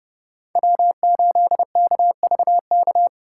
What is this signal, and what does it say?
Morse code, heard as a keyed tone, about 32 wpm, 700 Hz. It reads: W8KVK